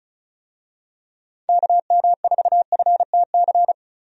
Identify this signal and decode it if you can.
Morse code, heard as a keyed tone, about 34 wpm, 700 Hz. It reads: KM4FTC
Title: USA Call Signs - 6 Characters - Repeated 3x - 35wpm